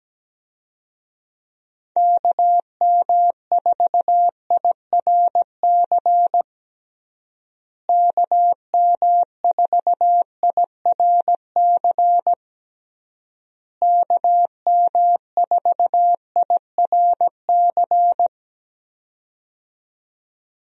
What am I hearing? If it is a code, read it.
Morse code, heard as a keyed tone, about 17 wpm, 700 Hz. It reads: KM4IRC KM4IRC KM4IRC
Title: USA Call Signs - 6 Characters - Repeated 3x - 17wpm